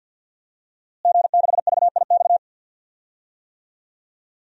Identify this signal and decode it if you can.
Morse code, heard as a keyed tone, about 47 wpm, 700 Hz. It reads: G6VIX
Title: DXCC-100 Call Signs - 5 Characters - Repeated 2x - 50wpm